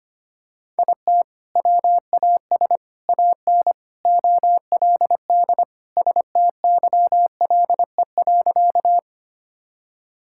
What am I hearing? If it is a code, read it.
Morse code, heard as a keyed tone, about 25 wpm, 700 Hz. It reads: IT WAS AN OLD STYLE.
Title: Sentences from Top 1000 Words - Review Part 5 - 25wpm